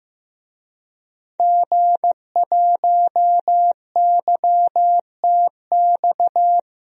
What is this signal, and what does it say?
Morse code, heard as a keyed tone, about 16 wpm, 700 Hz. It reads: G1YTX
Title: DXCC-100 Call Signs - 5 Characters - 15wpm